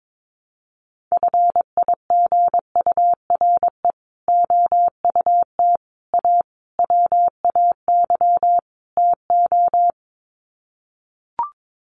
Morse code, heard as a keyed tone, about 22 wpm, 700 Hz. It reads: FIGURE OUT A WAY TO
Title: Sets of 5 Words Encore2 22wpm